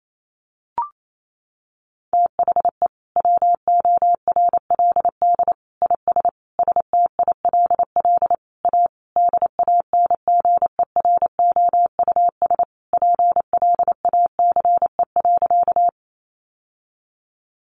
Morse code, heard as a keyed tone, about 28 wpm, 700 Hz. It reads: THE WORLD IS STILL A DANGEROUS PLACE.
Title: Sentences from Top 1200 Words 28wpm